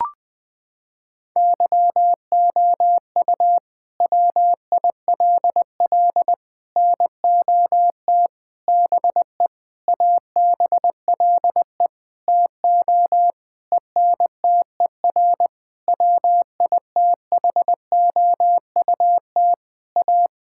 Morse code, heard as a keyed tone, about 21 wpm, 700 Hz. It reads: YOU WILL NOT BE ABLE TO ENTER WITHOUT A
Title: Sentences with 6-Word Phrases - 20wpm